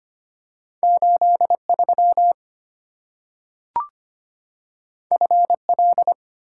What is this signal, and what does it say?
Morse code, heard as a keyed tone, about 25 wpm, 700 Hz. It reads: 83 FL
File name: Two Letter-Number Combinations - 25wpm